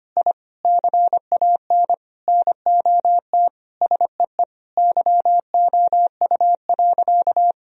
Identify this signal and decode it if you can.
Morse code, heard as a keyed tone, about 25 wpm, 700 Hz. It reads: I CAN NOT SEE YOU.
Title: Sentences from Top 1000 Words - Review Part 3 - 25wpm